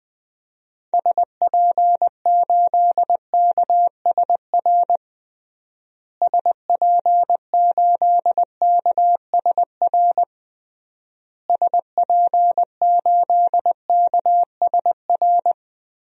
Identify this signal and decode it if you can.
Morse code, heard as a keyed tone, about 20 wpm, 700 Hz. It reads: SP8KSR SP8KSR SP8KSR
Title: DXCC-100 Call Signs - 6 Characters - Repeated 3x - 20wpm